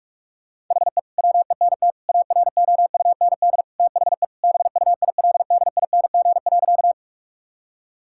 Morse code, heard as a keyed tone, about 46 wpm, 700 Hz. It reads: HE WENT AROUND THE BUILDING.
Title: Sentences from Top 1000 Words - Review Part 5 - 45wpm